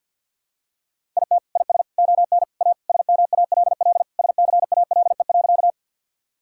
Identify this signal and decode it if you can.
Morse code, heard as a keyed tone, about 51 wpm, 690 Hz. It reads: IT IS ON A SMALL SCALE.
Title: Sentences from Top 1000 Words - Review Part 5 - 50wpm